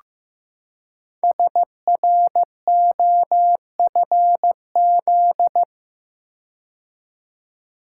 Morse code, heard as a keyed tone, about 15 wpm, 700 Hz. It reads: SROFZ
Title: Code Groups - 5 Letters - 15wpm